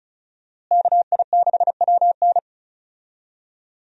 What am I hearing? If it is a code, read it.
Morse code, heard as a keyed tone, about 34 wpm, 700 Hz. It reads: KI6WN
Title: USA Call Signs - 5 Characters - Repeated 2x - 35wpm